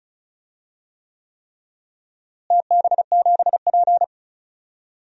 Morse code, heard as a keyed tone, about 34 wpm, 700 Hz. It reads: TB7P
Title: DXCC-100 Call Signs - 4 Characters - Repeated 3x - 35wpm